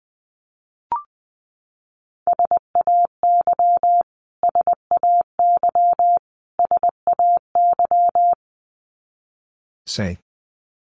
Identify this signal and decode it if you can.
Morse code, heard as a keyed tone, about 20 wpm, 700 Hz. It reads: SAY SAY SAY
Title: Top 500 Words Repeated 20wpm